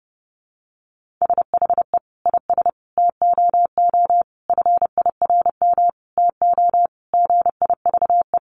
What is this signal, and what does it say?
Morse code, heard as a keyed tone, about 30 wpm, 700 Hz. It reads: SHE IS TOO FIRM TO GIVE
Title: Sentences from Top 1400 Words 30wpm